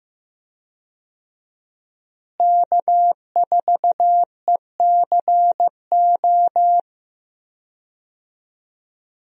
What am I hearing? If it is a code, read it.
Morse code, heard as a keyed tone, about 15 wpm, 700 Hz. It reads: K4ECO
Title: USA Call Signs - 5 Characters - Repeated 3x - 15wpm